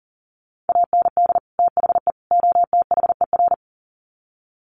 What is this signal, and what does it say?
Morse code, heard as a keyed tone, about 37 wpm, 700 Hz. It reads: AND THE OTHER
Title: Sets of 3 Words 40wpm